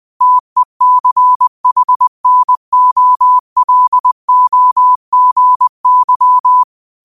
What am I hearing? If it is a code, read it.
Morse code, heard as a keyed tone, about 20 wpm, 1000 Hz. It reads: TECHNOLOGY